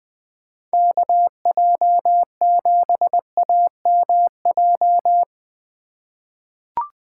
Morse code, heard as a keyed tone, about 20 wpm, 700 Hz. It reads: KJ7AMJ
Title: USA Call Signs - 6 Characters - 2x Character Spacing - 20wpm